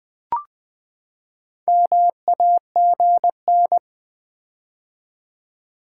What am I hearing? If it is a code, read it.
Morse code, heard as a keyed tone, about 20 wpm, 700 Hz. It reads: MAGN
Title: Build A Word - 4 Letters At A Time - 11-Letter Words - 20wpm